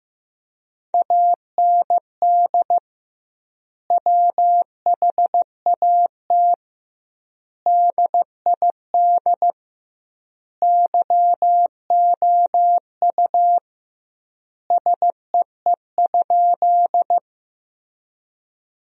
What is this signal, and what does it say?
Morse code, heard as a keyed tone, about 15 wpm, 700 Hz. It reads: AND WHAT DID YOU SEE?
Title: 2x Word-Spacing - Sentences from Top 100 Words - 15wpm